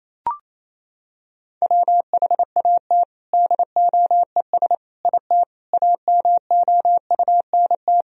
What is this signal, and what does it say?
Morse code, heard as a keyed tone, about 28 wpm, 700 Hz. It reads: WHAT DOES IT AMOUNT
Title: Sentences from Top 1000 Words - Review Part 4 - 28wpm